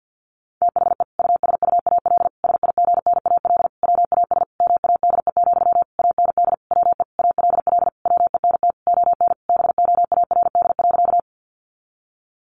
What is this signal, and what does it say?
Morse code, heard as a keyed tone, about 50 wpm, 700 Hz. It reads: THE USUAL SIGNAL WAS MADE, AND WE ALL WENT ON BOARD.
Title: Sentences from Top 1400 Words 50wpm